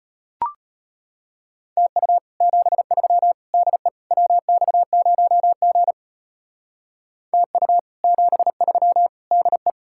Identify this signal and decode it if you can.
Morse code, heard as a keyed tone, about 32 wpm, 700 Hz. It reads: TU 73 DE WX0G TU 73 DE